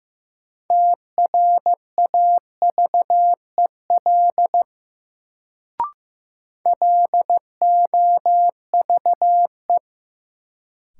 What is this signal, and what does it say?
Morse code, heard as a keyed tone, about 15 wpm, 700 Hz. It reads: TRAVEL LOVE
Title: Sentences from Top 400 Words - New Word Drill - 15wpm